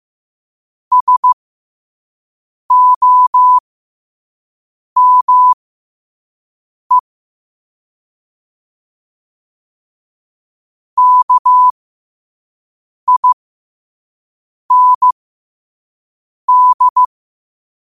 Morse code, heard as a keyed tone, about 15 wpm, spaced as for 5 wpm, 1000 Hz. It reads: SOME KIND
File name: qotdfiles / 5wpm